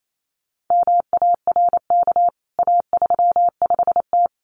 Morse code, heard as a keyed tone, about 28 wpm, 700 Hz. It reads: MARK A35T
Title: CWT Exchange 28wpm